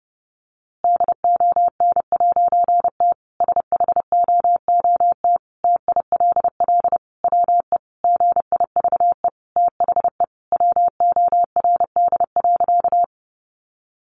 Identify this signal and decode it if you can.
Morse code, heard as a keyed tone, about 30 wpm, 700 Hz. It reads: DON'T SHOOT TILL WE GIVE THE WORD.
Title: A Sound Of Thunder 30wpm